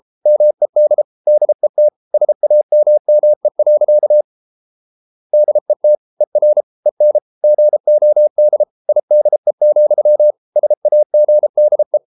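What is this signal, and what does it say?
Morse code, heard as a keyed tone, about 33 wpm, 600 Hz. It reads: MED DET SAMME. DET ER EN GOD IDE, SAGDE
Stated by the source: Morse code at 33 wpm, QRQ.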